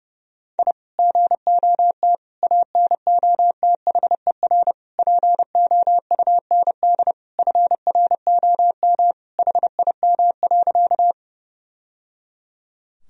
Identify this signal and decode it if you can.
Morse code, heard as a keyed tone, about 32 wpm, 700 Hz. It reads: I GOT ANOTHER POUND FROM HIM.
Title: Sentences from Top 1000 Words - Review Part 2 - 30wpm